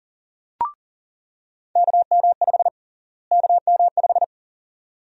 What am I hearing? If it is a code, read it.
Morse code, heard as a keyed tone, about 40 wpm, 700 Hz. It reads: KM5 KM5
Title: USA Call Sign Prefixes - Repeated 2x - 40wpm